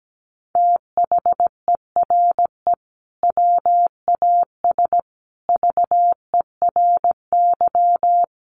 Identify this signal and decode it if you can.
Morse code, heard as a keyed tone, about 17 wpm, 700 Hz. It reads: THERE WAS VERY
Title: Sentences from Top 1400 Words 17wpm